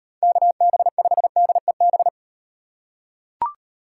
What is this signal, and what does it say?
Morse code, heard as a keyed tone, about 38 wpm, 700 Hz. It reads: KB5DEB